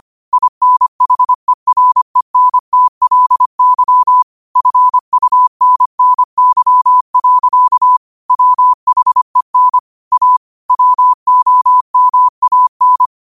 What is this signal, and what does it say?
Morse code, heard as a keyed tone, about 25 wpm, 1000 Hz. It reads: INHERENTLY FUNNY. WHEN A WOMAN
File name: qotdfiles / 25wpm